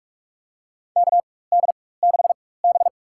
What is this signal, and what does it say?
Morse code, heard as a keyed tone, about 46 wpm, spaced as for 21 wpm, 700 Hz. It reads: KD6B
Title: USA Call Signs - 4 Characters - 4x Character Spacing - 45wpm